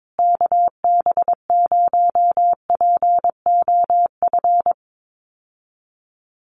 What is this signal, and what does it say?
Morse code, heard as a keyed tone, about 22 wpm, 700 Hz. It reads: KB0POF
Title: USA Call Signs Repeated 2x 22wpm